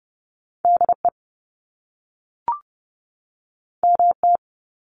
Morse code, heard as a keyed tone, about 30 wpm, 700 Hz. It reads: DE MT
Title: US States 30wpm